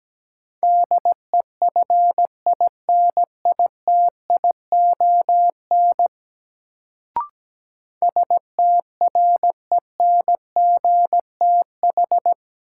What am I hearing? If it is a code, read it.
Morse code, heard as a keyed tone, about 17 wpm, 700 Hz. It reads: DEFINITION STRENGTH
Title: Sentences from Top 1000 Words - Review Part 4 - New Word Drill - 17wpm